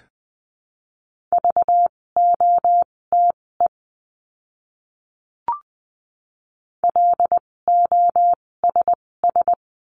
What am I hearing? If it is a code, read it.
Morse code, heard as a keyed tone, about 20 wpm, spaced as for 16 wpm, 700 Hz. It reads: VOTE LOSS
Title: 256 Lesson 42a - Words 12-20wpm